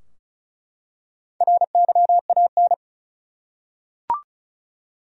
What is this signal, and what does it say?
Morse code, heard as a keyed tone, about 36 wpm, 700 Hz. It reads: RYAN
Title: USA Common Names - Max 4 Letters - 35wpm